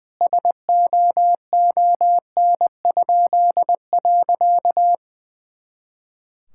Morse code, heard as a keyed tone, about 21 wpm, 700 Hz. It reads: SOON?.
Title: Pangram Challenge v2 - 20wpm